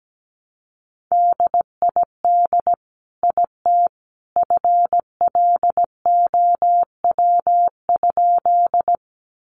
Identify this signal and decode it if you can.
Morse code, heard as a keyed tone, about 17 wpm, 700 Hz. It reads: DID IT FLOW?
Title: Sentences from Top 1200 Words 17wpm